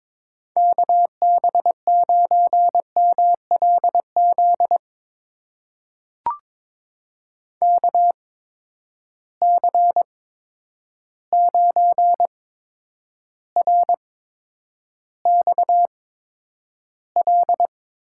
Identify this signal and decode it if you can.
Morse code, heard as a keyed tone, about 22 wpm, spaced as for 9 wpm, 700 Hz. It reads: KB9MLZ KC9RXL
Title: USA Call Signs - 8x Character Spacing - 22wpm